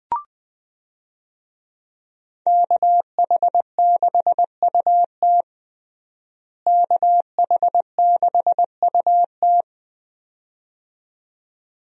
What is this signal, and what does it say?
Morse code, heard as a keyed tone, about 20 wpm, 700 Hz. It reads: KH6UT KH6UT
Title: USA Call Signs - 5 Characters - Repeated 2x - 20wpm